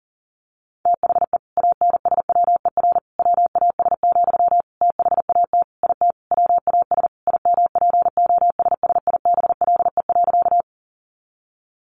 Morse code, heard as a keyed tone, about 40 wpm, 700 Hz. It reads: THE ANSWER WAS, THAT IT WAS IMPOSSIBLE.
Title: Sentences from Top 1100 Words 40wpm